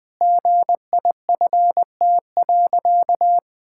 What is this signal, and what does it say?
Morse code, heard as a keyed tone, about 20 wpm, 700 Hz. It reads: GIFT.
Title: Sentences with 5-Word Phrases - 20wpm